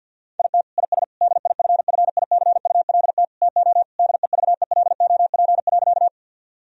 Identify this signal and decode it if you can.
Morse code, heard as a keyed tone, about 50 wpm, 700 Hz. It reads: IT IS DIFFICULT TO DEVELOP.
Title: Sentences from Top 1000 Words - Review Part 2 - 50wpm